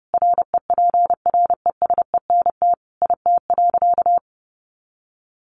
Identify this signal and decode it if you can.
Morse code, heard as a keyed tone, about 30 wpm, 700 Hz. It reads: REPRESENT IT.
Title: Sentences from Top 1200 Words 30wpm